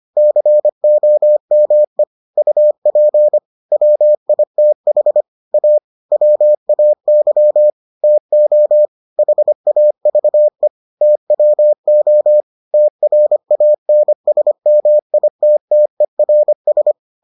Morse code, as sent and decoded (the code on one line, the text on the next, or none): -.-. --- -- . / ..- .--. / .-- .. - .... / .- / .-- .- -.-- / - --- / .... .- ...- . / - .-- --- / - .-. .- -. ... -- .. - - . .-. ...
COME UP WITH A WAY TO HAVE TWO TRANSMITTERS